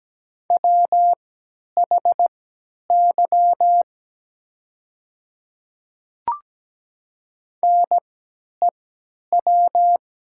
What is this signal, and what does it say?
Morse code, heard as a keyed tone, about 17 wpm, spaced as for 10 wpm, 700 Hz. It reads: WHY NEW
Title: 3x Character Spacing - Top 300 Words - 17wpm